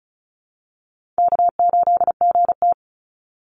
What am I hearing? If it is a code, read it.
Morse code, heard as a keyed tone, about 34 wpm, 700 Hz. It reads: K8GT
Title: Call Signs - 35wpm